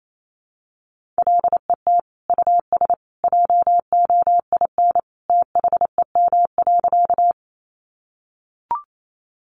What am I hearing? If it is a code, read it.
Morse code, heard as a keyed tone, about 28 wpm, 700 Hz. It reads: LET US JOIN THEM.
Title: Sentences from Top 1400 Words 28wpm